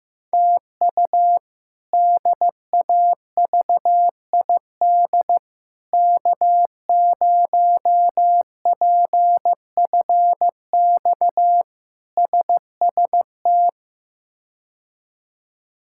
Morse code, heard as a keyed tone, about 15 wpm, 700 Hz. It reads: TU DAVID K0PFX SST
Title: K1USN SST QSO Practice - 15wpm